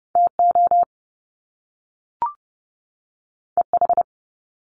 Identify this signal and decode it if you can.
Morse code, heard as a keyed tone, about 30 wpm, 700 Hz. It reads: TO EH
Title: Top Two Letter Combinations in Words 30wpm